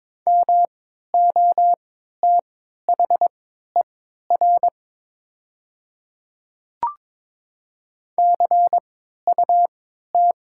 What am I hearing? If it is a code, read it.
Morse code, heard as a keyed tone, about 22 wpm, spaced as for 13 wpm, 700 Hz. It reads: MOTHER CUT
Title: 3x Character Spacing - Top 300 Words - 22wpm